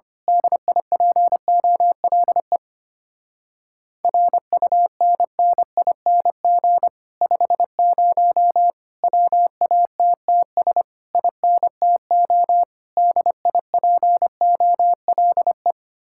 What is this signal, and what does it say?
Morse code, heard as a keyed tone, about 24 wpm, 700 Hz. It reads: DIPOLE RUNNING 50 WATTS INTO DIPOLE
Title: Speed-Racing Common QSO Phrases - 25wpm